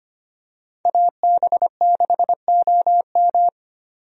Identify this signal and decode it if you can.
Morse code, heard as a keyed tone, about 25 wpm, 700 Hz. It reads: AB6OM
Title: USA Call Signs - 5 Characters - 25wpm